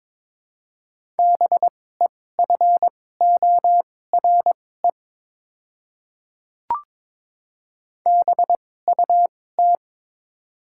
Morse code, heard as a keyed tone, about 21 wpm, spaced as for 16 wpm, 700 Hz. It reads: BEFORE BUT
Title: 2x Character Spacing - Top 200 Words - 22wpm